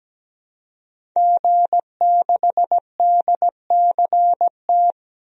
Morse code, heard as a keyed tone, about 17 wpm, 700 Hz. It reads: G6DCT
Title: DXCC-100 Call Signs - 5 Characters - 17wpm